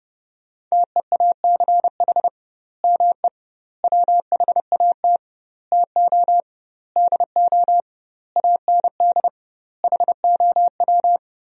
Morse code, heard as a keyed tone, about 32 wpm, 700 Hz. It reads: TEACH ME WHAT TO DO AND HOW
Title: 2x Word-Spacing - Sentences from Top 500 Words - 30wpm